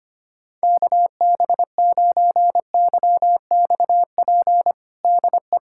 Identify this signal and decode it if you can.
Morse code, heard as a keyed tone, about 25 wpm, 700 Hz. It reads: KB9YXP DE
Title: USA Callsign DE Callsign 25wpm